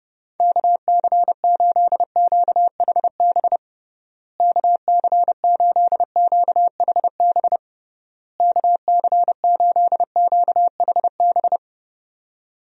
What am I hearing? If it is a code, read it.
Morse code, heard as a keyed tone, about 32 wpm, 700 Hz. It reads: KC8QHB KC8QHB KC8QHB
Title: USA Call Signs - 6 Characters - Repeated 3x - 30wpm